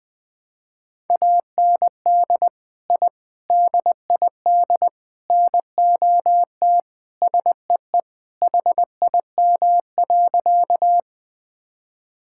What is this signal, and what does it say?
Morse code, heard as a keyed tone, about 21 wpm, 700 Hz. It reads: AND I DID NOT SEE HIM.
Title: Sentences with 4-Word Phrases - 20wpm